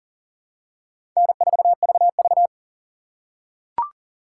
Morse code, heard as a keyed tone, about 39 wpm, 700 Hz. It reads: N4VV
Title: USA Call Signs - 4 Characters - 4x Character Spacing - 40wpm